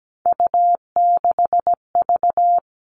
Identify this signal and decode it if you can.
Morse code, heard as a keyed tone, about 17 wpm, 700 Hz. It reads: U6V
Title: All Cut Numbers - 3-Digit 17wpm